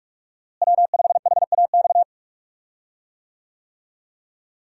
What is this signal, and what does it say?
Morse code, heard as a keyed tone, about 47 wpm, 700 Hz. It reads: W5HAX
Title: USA Call Signs - 5 Characters - Repeated 3x - 45wpm